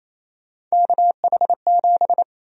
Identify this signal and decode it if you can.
Morse code, heard as a keyed tone, about 28 wpm, 700 Hz. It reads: KH7
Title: USA Call Sign Prefixes - 28wpm